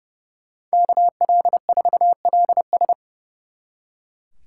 Morse code, heard as a keyed tone, about 39 wpm, 700 Hz. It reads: KL4LS